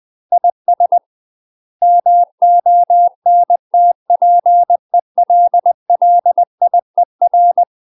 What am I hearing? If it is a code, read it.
Morse code, heard as a keyed tone, about 20 wpm, 700 Hz. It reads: IS MONTPELLIER
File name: UK MASTER QSOs TRAINING / 20WPM / UK MASTER QSOs 0053